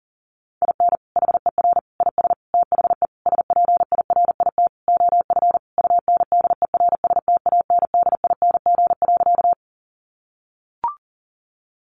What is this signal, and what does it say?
Morse code, heard as a keyed tone, about 40 wpm, 700 Hz. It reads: IN HER IS THE SPIRIT OF UNDERSTANDING.
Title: Sentences from Top 1400 Words 40wpm